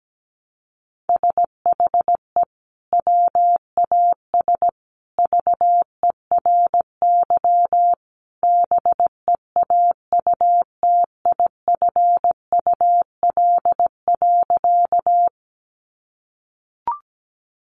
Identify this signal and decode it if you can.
Morse code, heard as a keyed tone, about 17 wpm, 700 Hz. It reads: SHE WAS VERY BEAUTIFUL.
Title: Sentences from Top 900 Words 17wpm